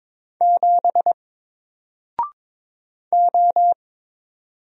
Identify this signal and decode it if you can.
Morse code, heard as a keyed tone, about 22 wpm, 700 Hz. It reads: 7 O
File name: Single-Letter-Number-22wpm